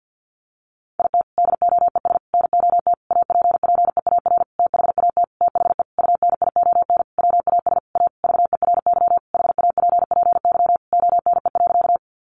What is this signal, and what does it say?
Morse code, heard as a keyed tone, about 50 wpm, 700 Hz. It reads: IT DOES NOT APPEAR THAT THE UNION WAS A VERY HAPPY ONE.
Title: Sentences from Top 1400 Words 50wpm